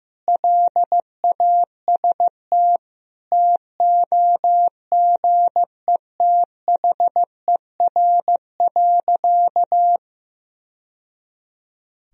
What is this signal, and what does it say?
Morse code, heard as a keyed tone, about 15 wpm, 700 Hz. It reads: LAST TOGETHER.
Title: Sentences from Top 1000 Words - Review Part 1 - 15wpm